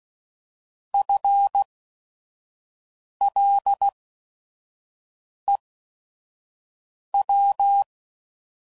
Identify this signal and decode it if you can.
Morse code, heard as a keyed tone, about 16 wpm, spaced as for 5 wpm, 800 Hz. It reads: FLEW
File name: Prefixes / By First Letter / Fs